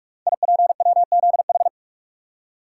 Morse code, heard as a keyed tone, about 44 wpm, 700 Hz. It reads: IPWZH
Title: Code Groups - 5 Letters - 45wpm